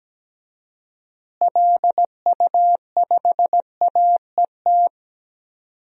Morse code, heard as a keyed tone, about 17 wpm, 700 Hz. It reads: LU5AET
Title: DXCC-100 Call Signs - 6 Characters - Repeated 2x - 17wpm